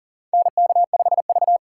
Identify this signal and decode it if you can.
Morse code, heard as a keyed tone, about 40 wpm, 700 Hz. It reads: NK5V
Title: USA Call Signs - 4 Characters - 6x Character Spacing - 40wpm